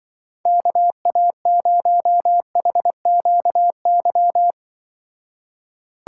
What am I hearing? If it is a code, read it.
Morse code, heard as a keyed tone, about 24 wpm, 700 Hz. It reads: KA0HQY
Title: Speed Racing - USA 2x3 Call Signs 20wpm